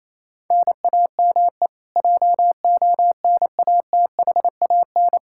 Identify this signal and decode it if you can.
Morse code, heard as a keyed tone, about 28 wpm, 700 Hz. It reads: NAME JONATHAN
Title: Practice QSO - Exchange 1 - Beginning - 28wpm